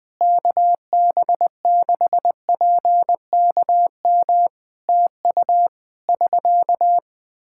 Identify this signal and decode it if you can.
Morse code, heard as a keyed tone, about 20 wpm, 700 Hz. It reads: KB6PKM TU <SK>
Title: Practice QSO - Exchange 4 - Ending - 20wpm